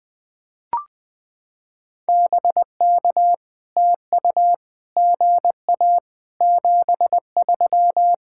Morse code, heard as a keyed tone, about 20 wpm, 700 Hz. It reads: BK TU GA 73
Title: POTA QSO Practice - 20wpm